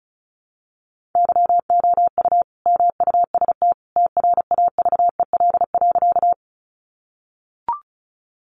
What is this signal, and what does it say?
Morse code, heard as a keyed tone, about 34 wpm, 700 Hz. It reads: YOU MUST TRAVEL.
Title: Sentences from Top 400 Words 35wpm